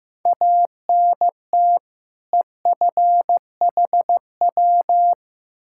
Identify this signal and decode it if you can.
Morse code, heard as a keyed tone, about 15 wpm, 700 Hz. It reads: ANT EFHW
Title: K1USN SST QSO Practice 15wpm